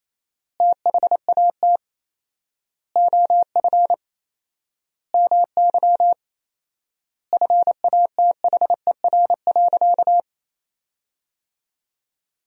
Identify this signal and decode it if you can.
Morse code, heard as a keyed tone, about 28 wpm, 700 Hz. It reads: THAT OF MY FATHER.
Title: 4x Word-Spacing - Sentences from Top 500 Words - 28wpm